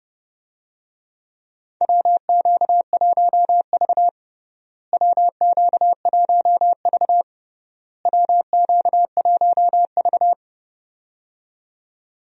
Morse code, heard as a keyed tone, about 30 wpm, 700 Hz. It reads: WQ1V WQ1V WQ1V
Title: USA Call Signs - 4 Characters - Repeated 3x - 30wpm